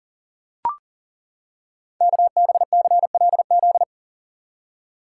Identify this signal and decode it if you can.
Morse code, heard as a keyed tone, about 39 wpm, 690 Hz. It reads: KBCLZ
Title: Code Groups - 5 Letters - 40wpm